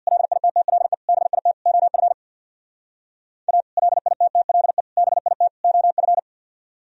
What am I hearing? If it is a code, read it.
Morse code, heard as a keyed tone, about 64 wpm, 700 Hz. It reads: LITTLE BIT OF A LITTLE BIT OF